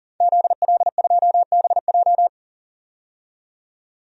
Morse code, heard as a keyed tone, about 40 wpm, 700 Hz. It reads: ZL2BJ
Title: DXCC-100 Call Signs - 5 Characters - 40wpm